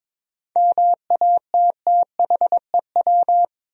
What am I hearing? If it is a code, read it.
Morse code, heard as a keyed tone, about 22 wpm, 700 Hz. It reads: MATTHEW
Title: USA Common Names - 22wpm